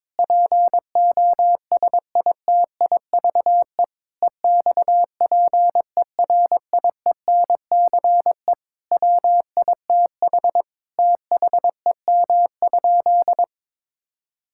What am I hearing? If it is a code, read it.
Morse code, heard as a keyed tone, about 22 wpm, 700 Hz. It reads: POSITIVE EXPERIENCE WITH THEM?
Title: Sentences from Top 1000 Words - Review Part 5 - 22wpm